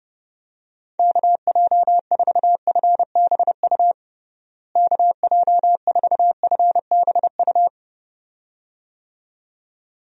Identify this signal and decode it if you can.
Morse code, heard as a keyed tone, about 30 wpm, 700 Hz. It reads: KJ4FBU KJ4FBU
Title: USA Call Signs - 6 Characters - Repeated 2x - 30wpm